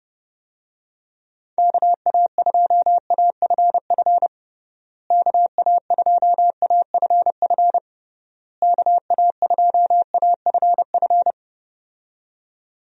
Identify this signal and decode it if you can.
Morse code, heard as a keyed tone, about 32 wpm, 700 Hz. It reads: KA2AFF KA2AFF KA2AFF
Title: USA Call Signs - 6 Characters - Repeated 3x - 30wpm